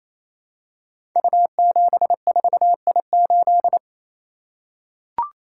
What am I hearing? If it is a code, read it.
Morse code, heard as a keyed tone, about 27 wpm, 700 Hz. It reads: U74I8
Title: Code Groups - 5 Letter:Numbers - 28wpm